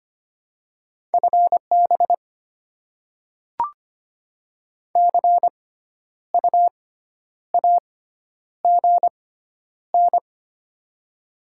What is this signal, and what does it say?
Morse code, heard as a keyed tone, about 25 wpm, spaced as for 10 wpm, 700 Hz. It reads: FB CU AGN
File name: Top 20 QSO Elements - 6x Character Spacing - 25wpm